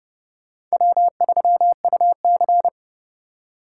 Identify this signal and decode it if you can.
Morse code, heard as a keyed tone, about 37 wpm, 700 Hz. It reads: W3UC
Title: Speed Racing - USA Call Signs 30wpm